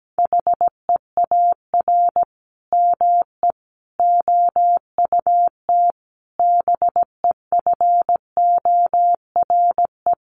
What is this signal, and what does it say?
Morse code, heard as a keyed tone, about 17 wpm, 700 Hz. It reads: HEAR ME OUT BEFORE
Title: Sentences from Top 1400 Words 17wpm